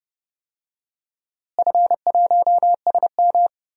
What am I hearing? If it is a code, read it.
Morse code, heard as a keyed tone, about 30 wpm, 700 Hz. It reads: F1SM
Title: DXCC-100 Call Signs - 4 Characters - Repeated 3x - 30wpm